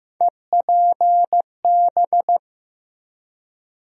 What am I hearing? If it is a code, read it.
Morse code, heard as a keyed tone, about 15 wpm, 700 Hz. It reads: EPB
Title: USA Call Sign Postfixes - 3 Letters 15wpm